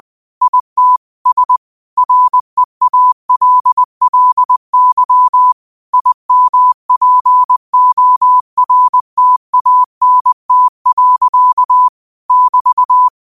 Morse code, heard as a keyed tone, about 20 wpm, 1000 Hz. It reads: IT S REALLY IMPORTANT. <BT>